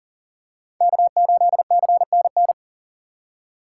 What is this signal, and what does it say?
Morse code, heard as a keyed tone, about 40 wpm, 700 Hz. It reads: K8CNN
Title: USA Call Signs - 5 Characters - 6x Character Spacing - 40wpm